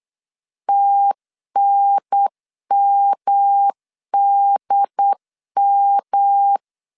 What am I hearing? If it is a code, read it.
Morse code, heard as a keyed tone, about 8 wpm, 800 Hz. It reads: TNMDM
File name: CW / DET / 1 3 1500C S B 8wpm 4141